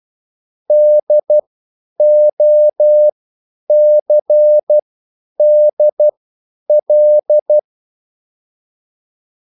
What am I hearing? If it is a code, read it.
Morse code, heard as a keyed tone, about 12 wpm, spaced as for 9 wpm, 600 Hz. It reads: DOCDL
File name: QSL / CW lesson